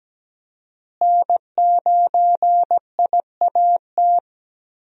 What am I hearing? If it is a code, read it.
Morse code, heard as a keyed tone, about 17 wpm, 700 Hz. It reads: N9IAT
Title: USA Call Signs - 5 Characters - Repeated 2x - 17wpm